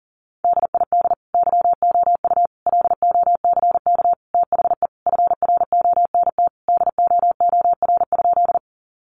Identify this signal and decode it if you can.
Morse code, heard as a keyed tone, about 40 wpm, 700 Hz. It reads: DID YOU LOCK THE FRONT DOOR?
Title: Sentences from Top 1300 Words 40wpm